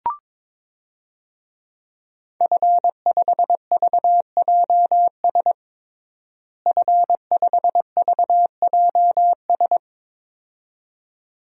Morse code, heard as a keyed tone, about 22 wpm, 700 Hz. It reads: F5VJS F5VJS
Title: DXCC-100 Call Signs - 5 Characters - Repeated 2x - 22wpm